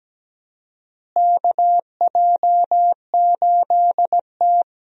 Morse code, heard as a keyed tone, about 17 wpm, 700 Hz. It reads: KJ8T
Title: USA Call Signs - 4 Characters - Repeated 2x - 17wpm